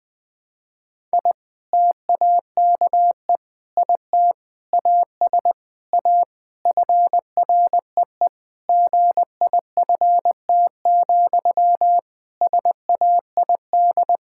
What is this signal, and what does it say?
Morse code, heard as a keyed tone, about 20 wpm, 700 Hz. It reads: I TAKE IT AS A FREE GIFT, SAID
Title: Sentences from Top 1000 Words - Review Part 6 - 20wpm